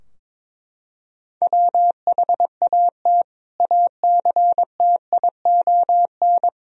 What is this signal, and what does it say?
Morse code, heard as a keyed tone, about 22 wpm, 700 Hz. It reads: WHAT ACTION
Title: Sentences from Top 1000 Words - Review Part 4 - 22wpm